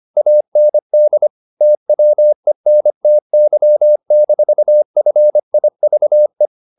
Morse code, heard as a keyed tone, about 25 wpm, 600 Hz. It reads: AND TWENTY-FIVE
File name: the old man and the sea part 1 25wpm